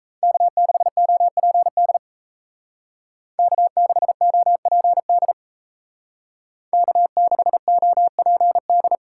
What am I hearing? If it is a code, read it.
Morse code, heard as a keyed tone, about 36 wpm, 700 Hz. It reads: K6OPD K6OPD K6OPD